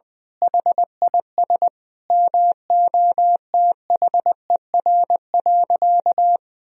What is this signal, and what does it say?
Morse code, heard as a keyed tone, about 20 wpm, 700 Hz. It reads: HIS MOTHER.
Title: Sentences from Top 1000 Words - Review Part 4 - 20wpm